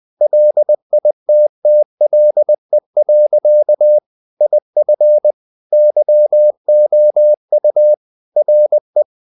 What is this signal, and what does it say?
Morse code, heard as a keyed tone, about 20 wpm, 600 Hz. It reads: LITTLE. IF YOU RE
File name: the old man and the sea / the old man and the sea part 1 20wpm